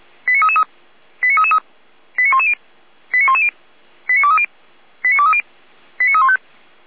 Five-tone selective call (ZVEI), heard as quick runs of five tones, tones of about 70 ms. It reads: 89333 89333 89110 89110 89230 89230 89315
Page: audio.f85360.feuerwehr-orlamuende.org